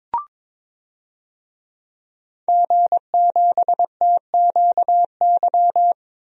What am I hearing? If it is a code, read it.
Morse code, heard as a keyed tone, about 22 wpm, 700 Hz. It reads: G7TQY
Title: DXCC-100 Call Signs - 5 Characters - Repeated 3x - 22wpm